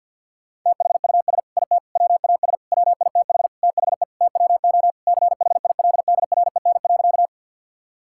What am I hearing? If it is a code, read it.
Morse code, heard as a keyed tone, about 48 wpm, 700 Hz. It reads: THUS IT WAS WITH THE TWO CHILDREN.
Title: Sentences from Top 1000 Words - Review Part 5 - 50wpm